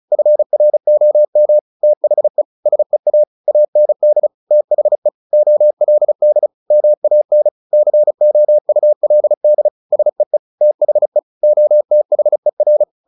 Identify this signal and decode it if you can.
Morse code, heard as a keyed tone, about 35 wpm, 600 Hz. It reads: FROM THE SEA AND THE OLD MAN COULD SEE THE OTHER
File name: the old man and the sea / the old man and the sea 1 35wpm